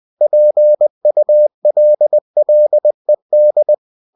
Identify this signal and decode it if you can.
Morse code, heard as a keyed tone, about 20 wpm, 600 Hz. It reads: PULLED
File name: cw-books / winnie-the-pooh / winnie-20wpm-page-0076